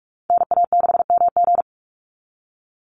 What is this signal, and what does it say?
Morse code, heard as a keyed tone, about 44 wpm, 700 Hz. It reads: NA6MG
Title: NA SOTA Activator Callsigns 45wpm